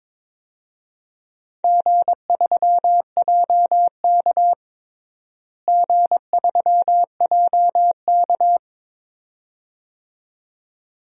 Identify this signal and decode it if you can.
Morse code, heard as a keyed tone, about 22 wpm, 700 Hz. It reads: G3JK G3JK
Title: DXCC-100 Call Signs - 4 Characters - Repeated 2x - 22wpm